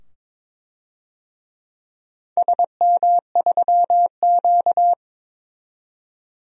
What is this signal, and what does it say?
Morse code, heard as a keyed tone, about 22 wpm, 700 Hz. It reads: SM3Q
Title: DXCC-100 Call Signs - 4 Characters - 22wpm